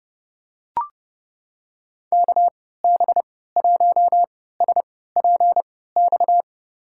Morse code, heard as a keyed tone, about 32 wpm, spaced as for 17 wpm, 700 Hz. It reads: KB1SPX
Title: USA Call Signs - 6 Characters - 3x Character Spacing - 30wpm